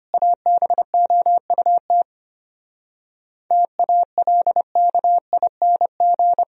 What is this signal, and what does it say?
Morse code, heard as a keyed tone, about 31 wpm, 700 Hz. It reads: ABOUT TALKING
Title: Speed-Racing - Sets of 2 Words 25wpm